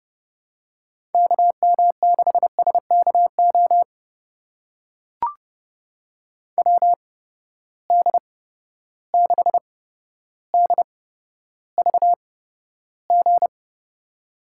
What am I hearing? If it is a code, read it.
Morse code, heard as a keyed tone, about 30 wpm, spaced as for 13 wpm, 700 Hz. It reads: KM6SKO WD6DVG
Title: USA Call Signs - 6 Characters - 8x Character Spacing - 30wpm